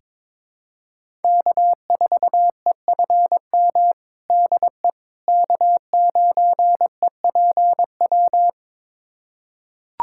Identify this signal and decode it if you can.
Morse code, heard as a keyed tone, about 22 wpm, 700 Hz. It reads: K4EFM DE K9EPW
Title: USA Callsign DE Callsign - 5 Characters - 22wpm